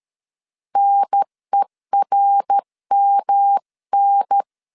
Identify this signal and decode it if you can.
Morse code, heard as a keyed tone, about 13 wpm, 800 Hz. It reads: NERMN
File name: CW / DET / 1 3 1500C S A 12wpm 2905